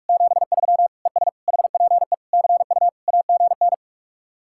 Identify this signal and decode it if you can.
Morse code, heard as a keyed tone, about 44 wpm, 700 Hz. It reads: 73 ES HPE CU AGN